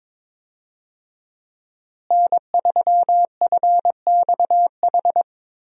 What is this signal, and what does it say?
Morse code, heard as a keyed tone, about 22 wpm, 700 Hz. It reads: N3FXH
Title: USA Call Signs - 5 Characters - Repeated 2x - 22wpm